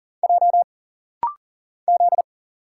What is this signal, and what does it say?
Morse code, heard as a keyed tone, about 40 wpm, 700 Hz. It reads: J Z
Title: Single Letters - Rapid-Fire 40wpm